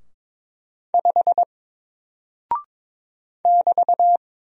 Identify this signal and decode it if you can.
Morse code, heard as a keyed tone, about 22 wpm, 700 Hz. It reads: 5 <BT>
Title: Single Character (Letter:Number:Punctuation:Prosign) Alt 22wpm